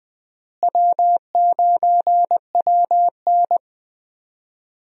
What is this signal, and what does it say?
Morse code, heard as a keyed tone, about 20 wpm, 700 Hz. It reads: W9WN
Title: USA Call Signs - 4 Characters - Repeated 3x - 20wpm